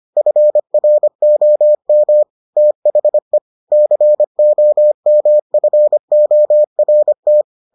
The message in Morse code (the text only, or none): FROM THE COMFORT